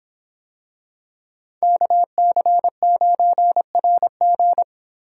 Morse code, heard as a keyed tone, about 26 wpm, 700 Hz. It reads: KC9RG